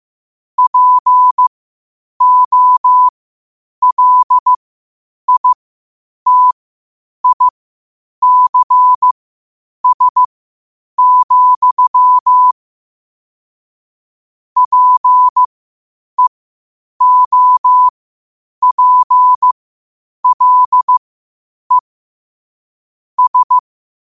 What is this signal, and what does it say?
Morse code, heard as a keyed tone, about 15 wpm, spaced as for 8 wpm, 1000 Hz. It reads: POLITICS, PEOPLE S
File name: qotdfiles / 8wpm